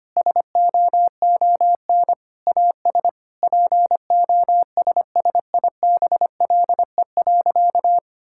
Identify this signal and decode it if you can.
Morse code, heard as a keyed tone, about 25 wpm, 700 Hz. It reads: SOON AS POSSIBLE.
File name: practice / 25wpm / Sentences with 3-Word Phrases - 25wpm